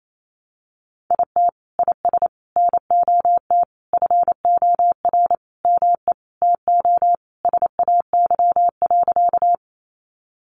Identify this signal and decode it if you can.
Morse code, heard as a keyed tone, about 28 wpm, 700 Hz. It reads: IT IS NOT FOR ME TO SAY.
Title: Sentences from Top 600 Words 28wpm